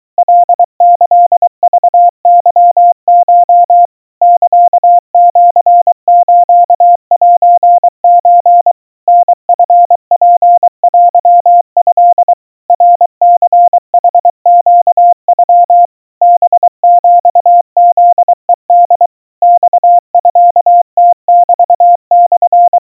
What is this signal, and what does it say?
Morse code (Japanese Wabun code), heard as a keyed tone, about 23 wpm, 700 Hz. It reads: カキクケコ サシスセソ タチツテト ナニヌネノ ハヒフヘホ マミムメモ
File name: kawa / Morse sample / aiueo 80 cpm 700 Hz Crispy